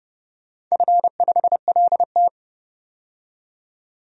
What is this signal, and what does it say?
Morse code, heard as a keyed tone, about 30 wpm, 700 Hz. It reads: F5LT
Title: DXCC-100 Call Signs - 4 Characters - 30wpm